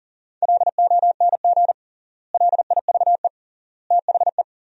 Morse code, heard as a keyed tone, about 40 wpm, 700 Hz. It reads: LONG LIVE THE
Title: 3x Word-Spacing - Sentences from Top 400 Words - 40wpm